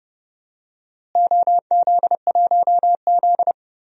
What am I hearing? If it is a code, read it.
Morse code, heard as a keyed tone, about 30 wpm, 700 Hz. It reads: OZ1Z
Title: DXCC-100 Call Signs - 4 Characters - Repeated 2x - 30wpm